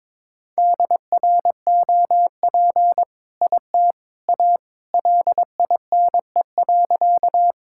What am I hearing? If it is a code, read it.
Morse code, heard as a keyed tone, about 22 wpm, 700 Hz. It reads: DROP IT A LINE.
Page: morse-code-ninja-practice-22wpm